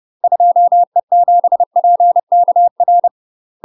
Morse code, heard as a keyed tone, about 30 wpm, 700 Hz. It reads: 2E7PKR
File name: UK MASTER QSOs TRAINING / 30WPM / UK MASTER QSOs 0031